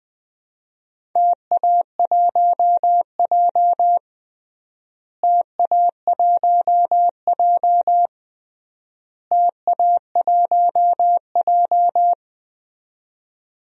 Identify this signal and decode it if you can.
Morse code, heard as a keyed tone, about 20 wpm, 700 Hz. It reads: TA1J TA1J TA1J
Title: DXCC-100 Call Signs - 4 Characters - Repeated 3x - 20wpm